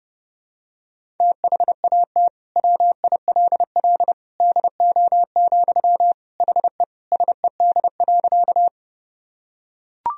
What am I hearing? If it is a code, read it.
Morse code, heard as a keyed tone, about 30 wpm, 700 Hz. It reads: THAT WILL DO, HE SED.
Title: Sentences from Top 100 Words and Abbreviations 30wpm